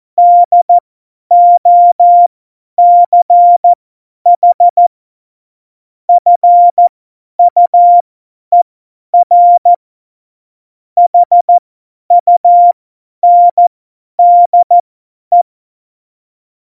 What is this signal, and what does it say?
Morse code, heard as a keyed tone, about 14 wpm, spaced as for 10 wpm, 700 Hz. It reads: DOCH FUER HUNDE
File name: Morsekurs / Witze / Witz 09. Oktober 2018, 50 F70